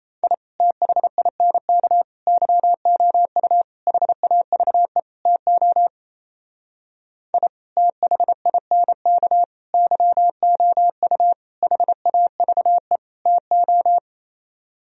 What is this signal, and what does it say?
Morse code, heard as a keyed tone, about 36 wpm, 700 Hz. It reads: I THINK YOU HAVE TO I THINK YOU HAVE TO